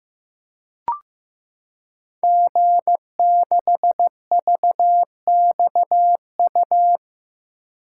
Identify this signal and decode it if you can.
Morse code, heard as a keyed tone, about 15 wpm, 700 Hz. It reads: G6VXU
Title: Code Groups - 5 Letter:Numbers - 15wpm